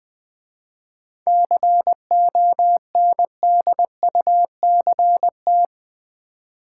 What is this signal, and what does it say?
Morse code, heard as a keyed tone, about 20 wpm, 700 Hz. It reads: CONDUCT
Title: 7 Letter Words - First 3 Letters Given - 20wpm